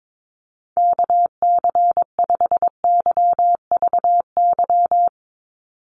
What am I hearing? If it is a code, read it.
Morse code, heard as a keyed tone, about 22 wpm, 700 Hz. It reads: KC5YVY
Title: USA Call Signs 22wpm